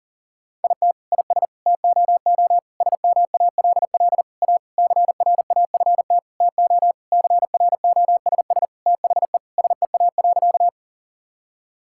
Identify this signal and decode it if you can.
Morse code, heard as a keyed tone, about 40 wpm, 700 Hz. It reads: IT IS TOO SMALL A CRAFT TO CROSS THE SEA.
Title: Sentences from Top 1000 Words - Review Part 6 - 40wpm